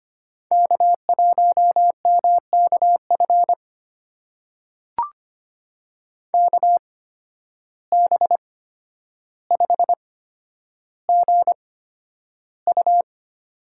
Morse code, heard as a keyed tone, about 25 wpm, spaced as for 11 wpm, 700 Hz. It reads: K1MKF KB5GU
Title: USA Call Signs - 5 Characters - 8x Character Spacing - 25wpm